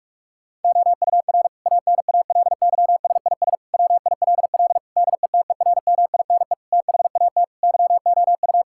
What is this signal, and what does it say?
Morse code, heard as a keyed tone, about 47 wpm, 700 Hz. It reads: OUR ANALYSIS WILL DETERMINE THAT YOU
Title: Sentences from Top 1000 Words - Review Part 5 - 45wpm